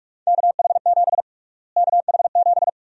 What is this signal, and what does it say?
Morse code, heard as a keyed tone, about 46 wpm, 690 Hz. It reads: KH7 KH7
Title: USA Call Sign Prefixes - Repeated 2x - 45wpm